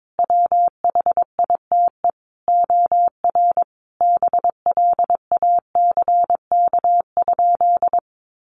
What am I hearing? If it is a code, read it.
Morse code, heard as a keyed tone, about 22 wpm, 700 Hz. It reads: WHITE OR BLACK?
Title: Sentences from Top 400 Words 22wpm